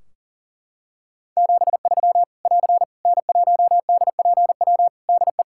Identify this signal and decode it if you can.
Morse code, heard as a keyed tone, about 37 wpm, 700 Hz. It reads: 73 <AR> N1DPW DE
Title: K1USN SST QSO Practice 40wpm